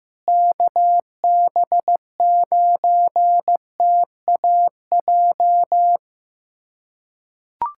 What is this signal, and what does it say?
Morse code, heard as a keyed tone, about 16 wpm, 700 Hz. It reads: KB9TAJ
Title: USA Callsign DE Callsign - 6 Characters - 15wpm